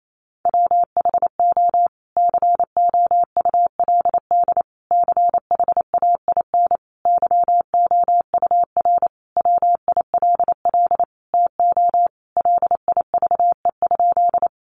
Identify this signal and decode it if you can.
Morse code, heard as a keyed tone, about 28 wpm, 700 Hz. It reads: WHO COULD CHAIN YOUR WILL TO LIVE?
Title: Sentences from Top 1500 Words 28wpm